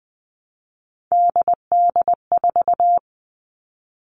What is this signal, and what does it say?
Morse code, heard as a keyed tone, about 20 wpm, 700 Hz. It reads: DD4
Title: All Cut Numbers - 3-Digit 20wpm